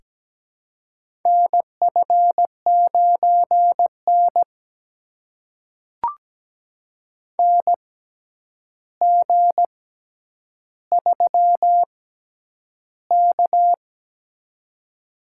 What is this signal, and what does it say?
Morse code, heard as a keyed tone, about 17 wpm, spaced as for 9 wpm, 700 Hz. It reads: NF9N NG3K
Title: USA Call Signs - 4 Characters - 6x Character Spacing - 17wpm